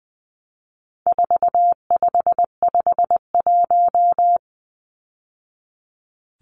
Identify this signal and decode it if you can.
Morse code, heard as a keyed tone, about 20 wpm, 700 Hz. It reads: 4551
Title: Numbers - 4-Digit 20wpm